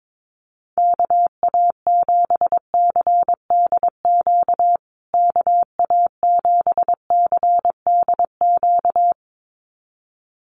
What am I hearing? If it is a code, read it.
Morse code, heard as a keyed tone, about 22 wpm, 700 Hz. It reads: KA7CDQ KA7CDQ
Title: USA Call Signs Repeated 2x 22wpm